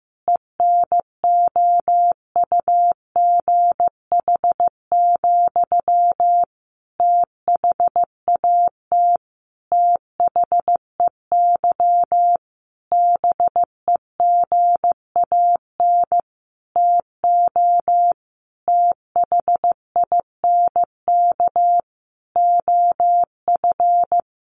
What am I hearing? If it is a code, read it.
Morse code, heard as a keyed tone, about 15 wpm, 700 Hz. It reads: ENOUGH, THAT THEY BEGAN TO THINK OF
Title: Sentences from Top 800 Words 15wpm